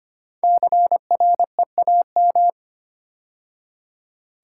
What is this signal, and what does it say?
Morse code, heard as a keyed tone, about 25 wpm, 700 Hz. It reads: CREAM
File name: 5 Letter Words - First 3 Letters Given - 25wpm